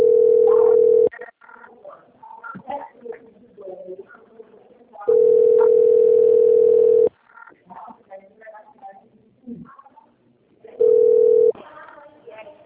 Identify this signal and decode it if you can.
Ringback tone (North America), 440 + 480 Hz, with speech.
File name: out- - - - - .47764